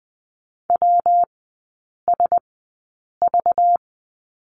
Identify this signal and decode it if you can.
Morse code, heard as a keyed tone, about 20 wpm, spaced as for 8 wpm, 700 Hz. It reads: WSV
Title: 191 Lesson 31e - Letter v - Three Characters 20wpm